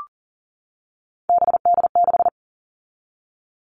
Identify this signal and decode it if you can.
Morse code, heard as a keyed tone, about 40 wpm, 700 Hz. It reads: BD6
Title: All Cut Numbers - 3-Digit 40wpm